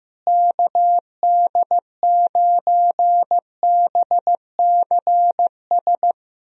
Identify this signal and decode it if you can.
Morse code, heard as a keyed tone, about 15 wpm, 700 Hz. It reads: KD9BCS
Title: USA Call Signs - 6 Characters - Repeated 3x - 15wpm